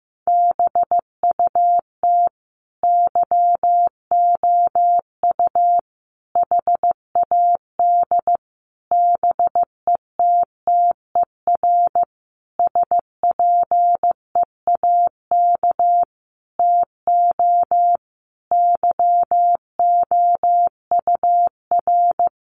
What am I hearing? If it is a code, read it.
Morse code, heard as a keyed tone, about 15 wpm, 700 Hz. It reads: BUT YOU HAD BETTER SPEAK TO YOUR
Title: Sentences from Top 1100 Words 15wpm